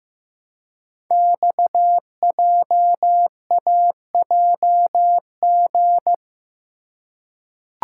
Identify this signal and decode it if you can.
Morse code, heard as a keyed tone, about 15 wpm, 700 Hz. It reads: XJAJG
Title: Code Groups - 5 Letters - 15wpm